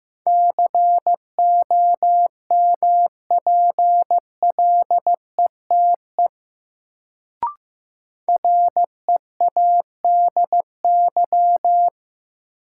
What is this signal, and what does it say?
Morse code, heard as a keyed tone, about 15 wpm, 700 Hz. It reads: COMPLETE READY
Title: Sentences from Top 400 Words - New Word Drill - 15wpm